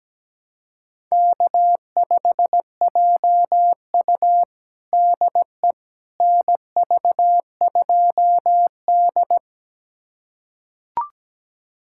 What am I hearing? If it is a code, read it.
Morse code, heard as a keyed tone, about 17 wpm, 700 Hz. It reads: K5JU DE NV2D
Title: USA Callsign DE Callsign - 4 Characters - 17wpm